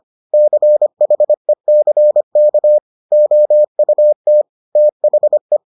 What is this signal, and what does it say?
Morse code, heard as a keyed tone, about 25 wpm, 600 Hz. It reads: CHECK OUT THE